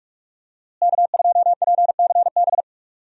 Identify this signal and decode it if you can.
Morse code, heard as a keyed tone, about 46 wpm, 700 Hz. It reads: K2PCB
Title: USA Call Signs - 5 Characters - Repeated 2x - 45wpm